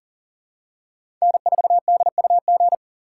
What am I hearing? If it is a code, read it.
Morse code, heard as a keyed tone, about 40 wpm, 700 Hz. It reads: N4DUG
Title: USA Call Signs - 5 Characters - 3x Character Spacing - 40wpm